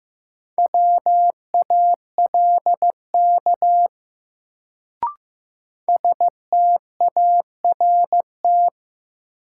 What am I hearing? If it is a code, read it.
Morse code, heard as a keyed tone, about 15 wpm, 700 Hz. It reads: WALK START
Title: Sentences from Top 300 Words - New Word Drill - 15wpm